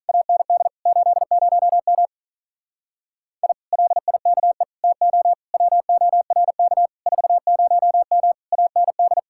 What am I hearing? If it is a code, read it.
Morse code, heard as a keyed tone, about 46 wpm, 700 Hz. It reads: AND 80M I LIKE TO WORK 40M AND